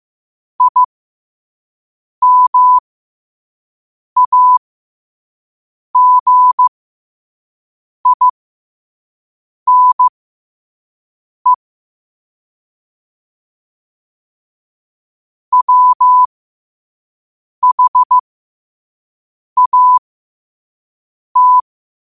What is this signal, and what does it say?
Morse code, heard as a keyed tone, about 15 wpm, spaced as for 5 wpm, 1000 Hz. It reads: IMAGINE WHAT
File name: qotdfiles / 5wpm